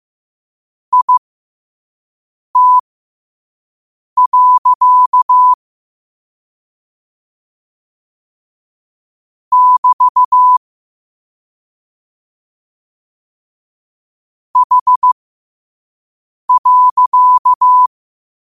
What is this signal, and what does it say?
Morse code, heard as a keyed tone, about 15 wpm, spaced as for 5 wpm, 1000 Hz. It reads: IT. <BT> H.
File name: qotdfiles / 5wpm